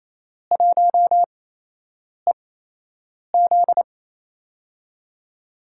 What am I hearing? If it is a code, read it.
Morse code, heard as a keyed tone, about 27 wpm, spaced as for 8 wpm, 700 Hz. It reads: 1EZ
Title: USA Call Signs - 8x Character Spacing - 28wpm